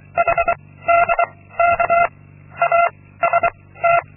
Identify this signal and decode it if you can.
Morse code, heard as a keyed tone, about 24 wpm, 680 Hz. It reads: H D K A S T
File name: Linksmakalnis-2